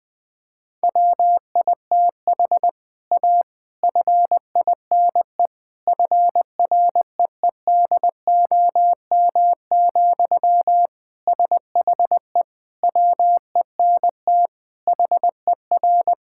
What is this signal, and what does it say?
Morse code, heard as a keyed tone, about 20 wpm, 700 Hz. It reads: WITH A FINE FREEDOM, SHE WENT HER
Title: Sentences from Top 1000 Words - Review Part 5 - 20wpm